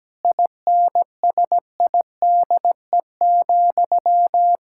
Morse code, heard as a keyed tone, about 17 wpm, 700 Hz. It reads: INSIDE,
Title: Sentences from Top 1000 Words - Review Part 6 - 17wpm